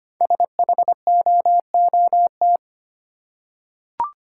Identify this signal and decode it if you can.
Morse code, heard as a keyed tone, about 25 wpm, 700 Hz. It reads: SHOOT
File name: 5 Letter Words - First 3 Letters Given - 25wpm